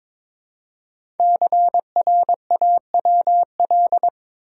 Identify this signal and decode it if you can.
Morse code, heard as a keyed tone, about 23 wpm, 700 Hz. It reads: CRAWL
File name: 5 Letter Words - First 3 Letters Given - 22wpm